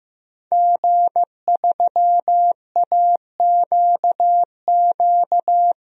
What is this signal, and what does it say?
Morse code, heard as a keyed tone, about 15 wpm, 700 Hz. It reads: G3AQQ
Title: DXCC-100 Call Signs - 5 Characters - 15wpm